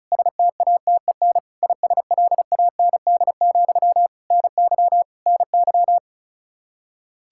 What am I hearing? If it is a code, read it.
Morse code, heard as a keyed tone, about 36 wpm, 700 Hz. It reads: STATEN ISLAND, NY NY
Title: Practice QSO - Exchange 1 - Beginning - 35wpm